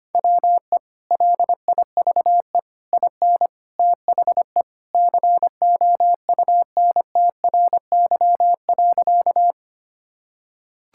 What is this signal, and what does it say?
Morse code, heard as a keyed tone, about 25 wpm, 700 Hz. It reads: WE LIVE IN THE COUNTRY.
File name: practice / 25wpm / Sentences with 3-Word Phrases - 25wpm